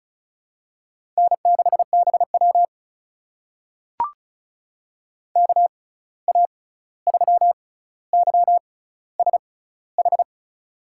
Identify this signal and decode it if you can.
Morse code, heard as a keyed tone, about 36 wpm, spaced as for 16 wpm, 700 Hz. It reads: N6BW KA3YSH
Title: USA Call Signs - 6x Character Spacing - 35wpm